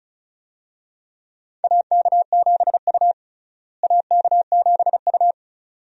Morse code, heard as a keyed tone, about 36 wpm, 700 Hz. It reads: AK7U AK7U
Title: USA Call Signs - 4 Characters - Repeated 2x - 35wpm